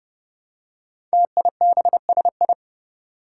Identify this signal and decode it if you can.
Morse code, heard as a keyed tone, about 28 wpm, 700 Hz. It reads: TIBSI
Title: Five Challenging Letter-Number Combinations 30wpm